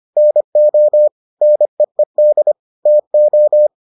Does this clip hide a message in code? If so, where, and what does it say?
Yes: Morse code from 0.2 to 3.7 s, heard as a keyed tone, about 25 wpm, 600 Hz. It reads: NO NEED TO